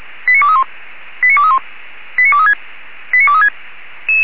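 Five-tone selective call (ZVEI), heard as quick runs of five tones, tones of about 70 ms. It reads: 89231 89231 89236 89236
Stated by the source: ZVEI five-tone selective call 89243, received on an SDR.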